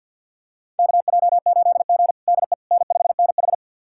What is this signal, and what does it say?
Morse code, heard as a keyed tone, about 45 wpm, 700 Hz. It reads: KJ8G DE N5NH
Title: USA Callsign DE Callsign - 4 Characters - 50wpm